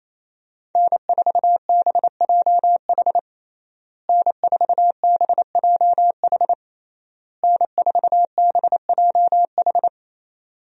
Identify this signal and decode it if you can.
Morse code, heard as a keyed tone, about 28 wpm, 700 Hz. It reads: N4BJH N4BJH N4BJH
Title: USA Call Signs - 5 Characters - Repeated 3x - 28wpm